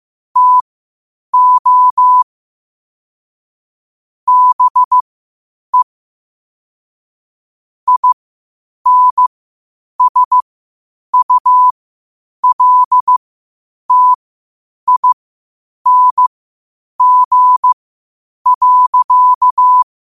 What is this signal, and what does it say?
Morse code, heard as a keyed tone, about 15 wpm, spaced as for 8 wpm, 1000 Hz. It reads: TO BE INSULTING.
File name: qotdfiles / 8wpm